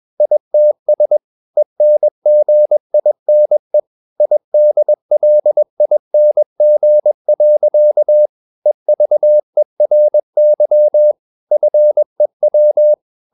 Morse code, heard as a keyed tone, about 21 wpm, 600 Hz. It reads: ITS ENGINE IDLING. EVERY FEW
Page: CW med Gnister